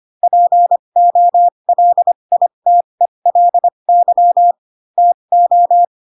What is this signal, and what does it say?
Morse code, heard as a keyed tone, about 25 wpm, spaced as for 22 wpm, 700 Hz. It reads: POLITELY TO